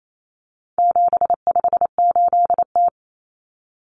Morse code, heard as a keyed tone, about 28 wpm, 700 Hz. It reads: 758T
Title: Common Cut Numbers - 4-Digit 28wpm